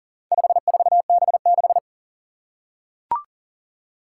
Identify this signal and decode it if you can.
Morse code, heard as a keyed tone, about 40 wpm, 700 Hz. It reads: 54B6
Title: Four Challenging Letter-Number Combinations 40wpm